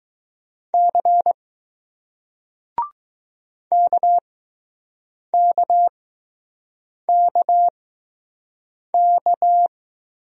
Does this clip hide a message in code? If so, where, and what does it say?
Yes: Morse code from 0.7 to 9.7 s, heard as a keyed tone, about 20 wpm, spaced as for 6 wpm, 700 Hz. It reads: C KKKK